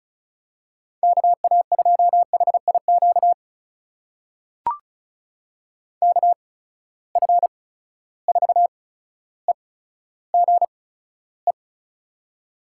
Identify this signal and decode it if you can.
Morse code, heard as a keyed tone, about 36 wpm, spaced as for 15 wpm, 700 Hz. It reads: KA2HIQ KF4EGE
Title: USA Call Signs - 8x Character Spacing - 35wpm